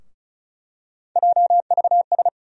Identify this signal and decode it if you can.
Morse code, heard as a keyed tone, about 36 wpm, 700 Hz. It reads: JVS
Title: USA Call Sign Postfixes - 3 Letters 35wpm